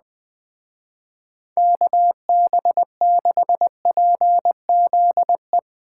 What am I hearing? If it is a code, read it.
Morse code, heard as a keyed tone, about 20 wpm, 700 Hz. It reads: KB6PZE
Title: Speed Racing - USA 2x3 Call Signs 17wpm